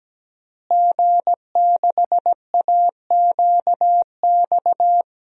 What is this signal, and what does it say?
Morse code, heard as a keyed tone, about 17 wpm, 700 Hz. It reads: G6AQX
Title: DXCC-100 Call Signs - 5 Characters - Repeated 3x - 17wpm